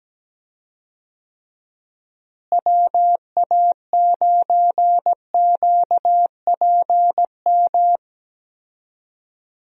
Morse code, heard as a keyed tone, about 17 wpm, 700 Hz. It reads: WA9QPM
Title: USA Call Signs - 6 Characters - Repeated 3x - 17wpm